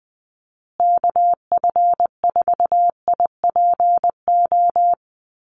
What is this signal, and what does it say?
Morse code, heard as a keyed tone, about 20 wpm, 700 Hz. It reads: KF4IPO
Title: USA Call Signs Repeated 2x 20wpm